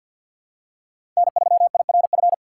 Morse code, heard as a keyed tone, about 53 wpm, 700 Hz. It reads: N3IRF
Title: USA Call Signs - 5 Characters - 8x Character Spacing - 50wpm